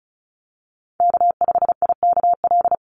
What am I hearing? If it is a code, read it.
Morse code, heard as a keyed tone, about 34 wpm, 700 Hz. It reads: K5IKL
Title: USA Call Signs 35wpm